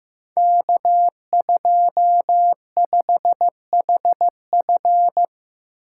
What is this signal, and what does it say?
Morse code, heard as a keyed tone, about 15 wpm, 700 Hz. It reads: K25HF
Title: Code Groups - 5 Letter:Numbers - 15wpm